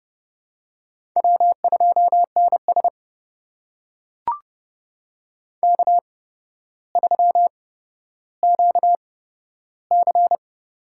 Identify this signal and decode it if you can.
Morse code, heard as a keyed tone, about 30 wpm, spaced as for 13 wpm, 700 Hz. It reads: W2NS K3QC
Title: USA Call Signs - 8x Character Spacing - 30wpm